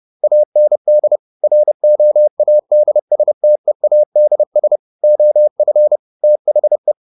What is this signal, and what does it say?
Morse code, heard as a keyed tone, about 30 wpm, 600 Hz. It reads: AND ROADSTEADS OF THE
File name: the old man and the sea / the old man and the sea 1 30wpm